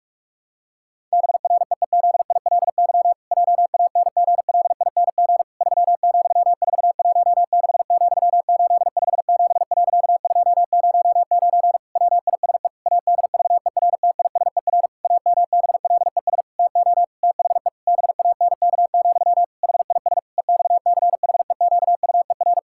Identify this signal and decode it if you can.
Morse code, heard as a keyed tone, about 41 wpm, 700 Hz. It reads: BREEZILY JANGLING 3,416,857.209 WISE ADVERTISER AMBLES TO THE BANK, HIS EXCHEQUER
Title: Pangram Challenge v2 - 45wpm